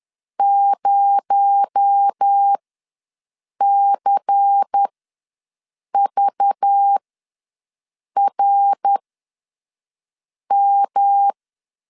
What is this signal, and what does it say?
Morse code, heard as a keyed tone, about 10 wpm, spaced as for 5 wpm, 800 Hz. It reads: 0CVRM
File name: LEZIONI / 9 2 ws 10wpm 4130